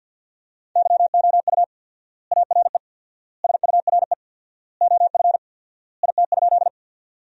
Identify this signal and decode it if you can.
Morse code, heard as a keyed tone, about 52 wpm, 700 Hz. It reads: YOU ARE SURE OF IT?
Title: 4x Word-Spacing - Sentences from Top 300 Words - 50wpm